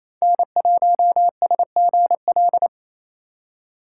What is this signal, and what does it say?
Morse code, heard as a keyed tone, about 28 wpm, 700 Hz. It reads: N1SGL
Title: USA Call Signs - 5 Characters - 4x Character Spacing - 28wpm